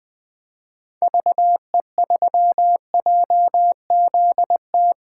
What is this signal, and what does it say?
Morse code, heard as a keyed tone, about 20 wpm, 700 Hz. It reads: VE3JZT
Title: 2023 POTA Activator Call Signs - 20wpm